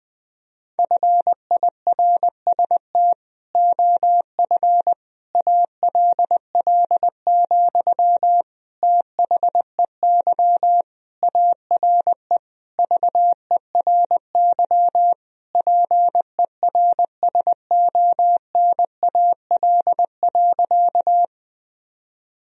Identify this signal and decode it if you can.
Morse code, heard as a keyed tone, about 20 wpm, 700 Hz. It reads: FIRST OF ALL, THEY ARE VERY PERSONAL.
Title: Sentences from Top 1000 Words - Review Part 4 - 20wpm